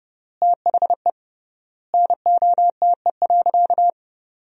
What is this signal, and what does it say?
Morse code, heard as a keyed tone, about 30 wpm, 700 Hz. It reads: THE NOTE.
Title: 3x Word-Spacing - Sentences from Top 500 Words - 30wpm